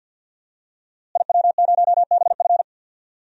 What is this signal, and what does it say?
Morse code, heard as a keyed tone, about 50 wpm, 700 Hz. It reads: IW0BF
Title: DXCC-100 Call Signs - 5 Characters - Repeated 2x - 50wpm